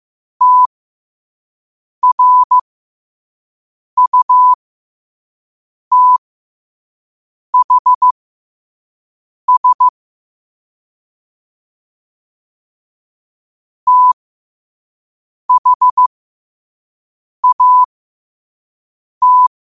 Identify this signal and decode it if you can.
Morse code, heard as a keyed tone, about 15 wpm, spaced as for 5 wpm, 1000 Hz. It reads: TRUTHS THAT
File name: qotdfiles / 5wpm